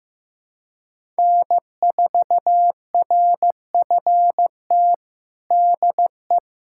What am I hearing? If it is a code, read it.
Morse code, heard as a keyed tone, about 15 wpm, 700 Hz. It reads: N4RFT DE
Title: USA Callsign DE Callsign 15wpm